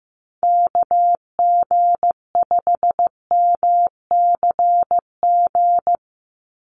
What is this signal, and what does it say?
Morse code, heard as a keyed tone, about 15 wpm, 700 Hz. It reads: KG5MCG
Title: USA Call Signs 15wpm